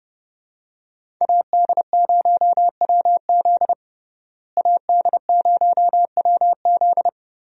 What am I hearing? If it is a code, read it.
Morse code, heard as a keyed tone, about 32 wpm, 700 Hz. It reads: AD0WZ AD0WZ
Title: USA Call Signs - 5 Characters - Repeated 2x - 30wpm